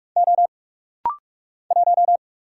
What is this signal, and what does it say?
Morse code, heard as a keyed tone, about 44 wpm, 700 Hz. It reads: O 1
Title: Single Letter-Number Alt - Rapid-Fire 45wpm